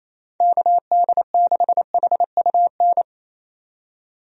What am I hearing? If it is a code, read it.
Morse code, heard as a keyed tone, about 27 wpm, 700 Hz. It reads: KD6HUN
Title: USA Call Signs - 6 Characters - 28wpm